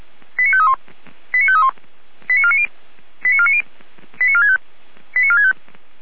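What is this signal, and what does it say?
Five-tone selective call (ZVEI), heard as quick runs of five tones, tones of about 70 ms. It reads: 89531 89531 89490 89490 89465 89465
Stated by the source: ZVEI five-tone selective call 89518 received on an SDR.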